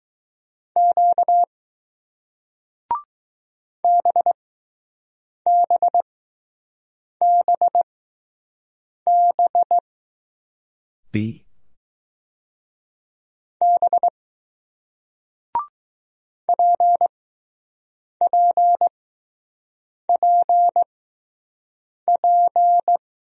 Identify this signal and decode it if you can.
Morse code, heard as a keyed tone, about 20 wpm, spaced as for 6 wpm, 700 Hz. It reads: Q BBBB B PPPP